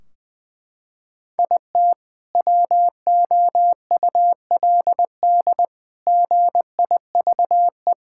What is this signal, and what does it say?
Morse code, heard as a keyed tone, about 20 wpm, 700 Hz. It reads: IT WOULD GIVE
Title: Sentences from Top 1600 Words 20wpm